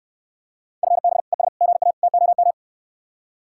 Morse code, heard as a keyed tone, about 69 wpm, 700 Hz. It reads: FB ES GA TOM